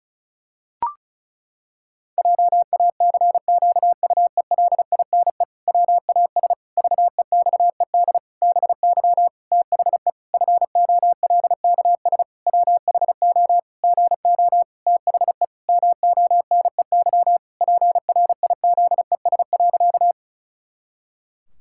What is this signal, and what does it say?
Morse code, heard as a keyed tone, about 34 wpm, 700 Hz. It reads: JACQUELINE WAS VEXED BY THE FOLKS WHO GO THE MONEY PRIZES.
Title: Pangram Challenge v2 - 35wpm